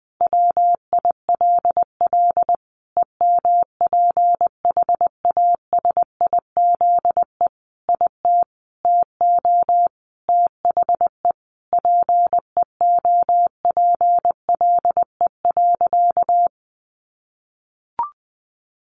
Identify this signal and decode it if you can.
Morse code, heard as a keyed tone, about 20 wpm, 700 Hz. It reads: WILL EMPHASIZE IT TO THE PEOPLE.
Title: Sentences from Top 1500 Words 20wpm